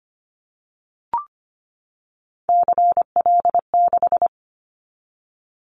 Morse code, heard as a keyed tone, about 25 wpm, 700 Hz. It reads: CL6
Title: Callsign Prefixes for Top 100 DXCC - 25wpm